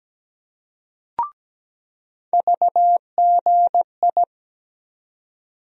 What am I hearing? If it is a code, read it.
Morse code, heard as a keyed tone, about 17 wpm, 700 Hz. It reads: VGI
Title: USA Call Sign Postfixes - 3 Letters 17wpm